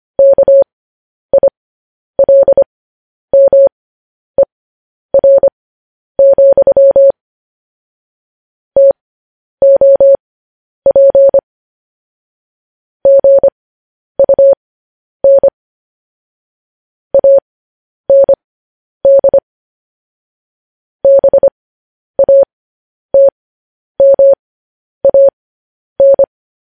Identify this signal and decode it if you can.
Morse code, heard as a keyed tone, about 25 wpm, spaced as for 10 wpm, 550 Hz. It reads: KILMER, TOP GUN AND BATMAN
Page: news headlines in morse code at 10 wpm